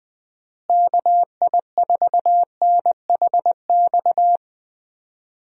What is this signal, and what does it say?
Morse code, heard as a keyed tone, about 20 wpm, 700 Hz. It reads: KI4NHX
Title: USA Call Signs - 6 Characters - Repeated 2x - 20wpm